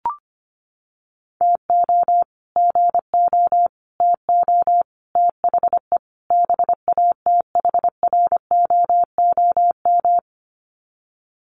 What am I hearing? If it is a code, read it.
Morse code, heard as a keyed tone, about 25 wpm, 700 Hz. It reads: TO GO TO THE BATHROOM
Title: Sets of 5 Words Encore 25wpm